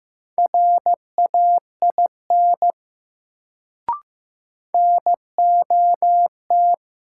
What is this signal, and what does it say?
Morse code, heard as a keyed tone, about 15 wpm, 700 Hz. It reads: RAIN NOT
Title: Sentences from Top 600 Words - New Word Drill - 15wpm